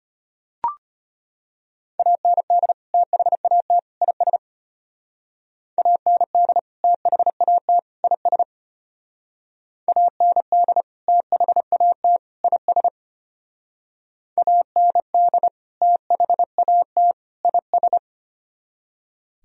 Morse code, heard as a keyed tone, about 30 wpm, 700 Hz. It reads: AND THAT IS AND THAT IS AND THAT IS AND THAT IS
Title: Speed-Racing - Sets of 3 Words 25wpm